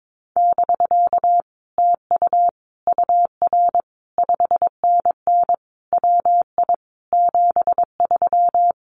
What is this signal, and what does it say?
Morse code, heard as a keyed tone, about 22 wpm, 700 Hz. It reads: <BK> TU UR 5NN WI 73
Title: Common QSO Phrases Encore 22wpm